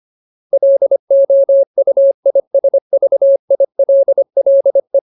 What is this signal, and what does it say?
Morse code, heard as a keyed tone, about 25 wpm, 550 Hz. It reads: LOUISVILLE
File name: Ultimate Word List / 25WPM Ultimate